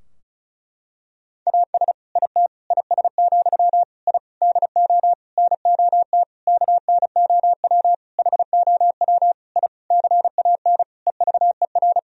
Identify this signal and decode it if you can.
Morse code, heard as a keyed tone, about 36 wpm, 700 Hz. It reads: AS IT IS, I DO NOT KNOW HOW I CAN EVER
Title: Sentences from Top 1000 Words - Review Part 2 - 35wpm